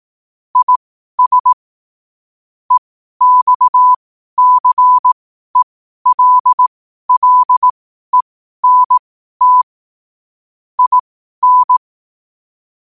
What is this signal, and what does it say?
Morse code, heard as a keyed tone, about 18 wpm, spaced as for 12 wpm, 1000 Hz. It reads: IS EXCELLENT IN
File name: qotdfiles / 12wpm